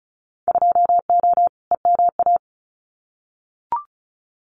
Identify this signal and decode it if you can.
Morse code, heard as a keyed tone, about 34 wpm, 700 Hz. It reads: 2O EMA
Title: Winter Field Day Exchanges 35wpm